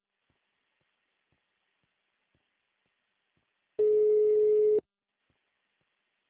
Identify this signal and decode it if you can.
Ringback tone (Europe, 425 Hz).
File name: out- -2007- - - .69